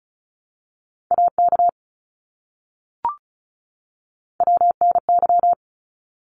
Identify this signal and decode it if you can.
Morse code, heard as a keyed tone, about 34 wpm, 700 Hz. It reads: AK WNY
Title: ARRL Sections 35wpm